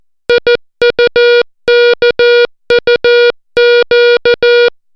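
Morse code, heard as a keyed tone, about 14 wpm, 480 Hz. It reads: IUKUQ
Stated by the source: Morse alphabet lesson, 60 characters per minute.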